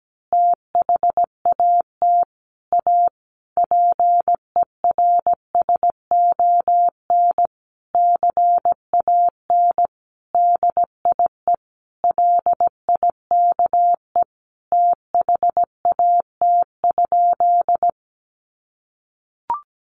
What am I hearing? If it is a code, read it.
Morse code, heard as a keyed tone, about 17 wpm, 700 Hz. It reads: THAT A PERSON CAN DIE LIKE THAT?
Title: Sentences from Top 1400 Words 17wpm